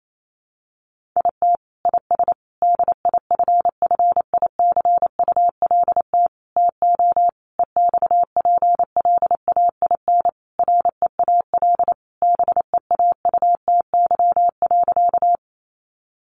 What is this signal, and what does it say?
Morse code, heard as a keyed tone, about 28 wpm, 700 Hz. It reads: IT IS DIFFICULT TO EXPLAIN REAL BEAUTY.
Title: Sentences from Top 1100 Words 28wpm